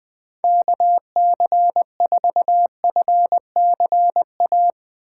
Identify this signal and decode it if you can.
Morse code, heard as a keyed tone, about 21 wpm, 700 Hz. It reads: KC4FCA
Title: Speed Racing - USA 2x3 Call Signs 17wpm